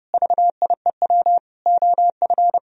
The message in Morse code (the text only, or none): VIEW OF